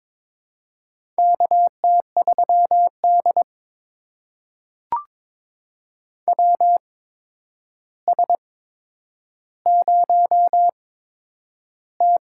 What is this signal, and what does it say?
Morse code, heard as a keyed tone, about 22 wpm, spaced as for 9 wpm, 700 Hz. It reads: KT3D WS0T
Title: USA Call Signs - 4 Characters - 8x Character Spacing - 22wpm